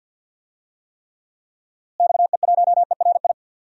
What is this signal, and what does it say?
Morse code, heard as a keyed tone, about 48 wpm, 700 Hz. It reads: XE1ERI
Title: DXCC-100 Call Signs - 6 Characters - Repeated 3x - 50wpm